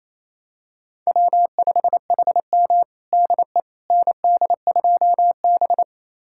Morse code, heard as a keyed tone, about 28 wpm, 700 Hz. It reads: W5HM DE ND2B
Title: USA Callsign DE Callsign - 4 Characters - 28wpm